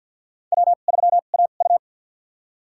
Morse code, heard as a keyed tone, about 50 wpm, spaced as for 36 wpm, 700 Hz. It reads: W3AU
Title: USA Call Signs - 2x Character Spacing - 50wpm